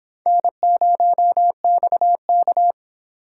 Morse code, heard as a keyed tone, about 29 wpm, 700 Hz. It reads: N0XK